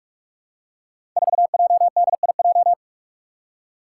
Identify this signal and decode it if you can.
Morse code, heard as a keyed tone, about 44 wpm, 700 Hz. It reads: 4JDIJ
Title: Five Challenging Letter-Number Combinations 45wpm